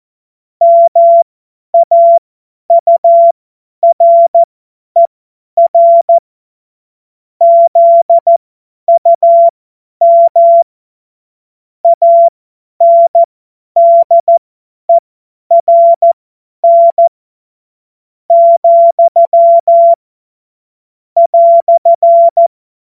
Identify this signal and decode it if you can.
Morse code, heard as a keyed tone, about 14 wpm, spaced as for 9 wpm, 680 Hz. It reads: MAURER ZUM ANDERN, "
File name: Morsekurs / Witze / Witz 16. Dezember 2025 50 F70